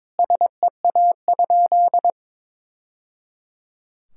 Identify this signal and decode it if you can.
Morse code, heard as a keyed tone, about 22 wpm, 700 Hz. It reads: SEA?
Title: Sentences from Top 1000 Words - Review Part 6 - 22wpm